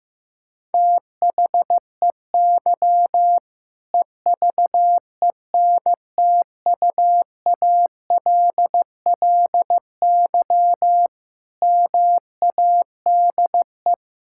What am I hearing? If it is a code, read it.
Morse code, heard as a keyed tone, about 16 wpm, 700 Hz. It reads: THEY EVENTUALLY MADE
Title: Sentences from Top 1000 Words - Review Part 6 - 15wpm